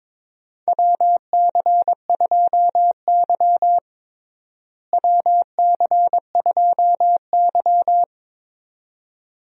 Morse code, heard as a keyed tone, about 22 wpm, 700 Hz. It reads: WC2Y WC2Y
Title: USA Call Signs - 4 Characters - Repeated 2x - 22wpm